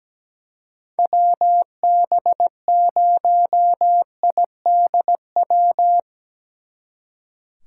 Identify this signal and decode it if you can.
Morse code, heard as a keyed tone, about 18 wpm, 700 Hz. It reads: WB0IDW
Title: Speed Racing - USA 2x3 Call Signs 17wpm